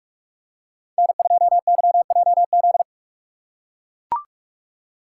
Morse code, heard as a keyed tone, about 44 wpm, 700 Hz. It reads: N2YJZ